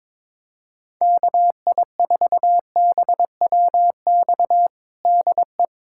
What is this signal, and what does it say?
Morse code, heard as a keyed tone, about 22 wpm, 700 Hz. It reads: KI4BWX DE
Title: USA Callsign DE Callsign - 6 Characters - 22wpm